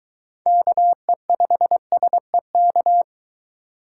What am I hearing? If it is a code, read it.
Morse code, heard as a keyed tone, about 23 wpm, 700 Hz. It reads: KE5SEK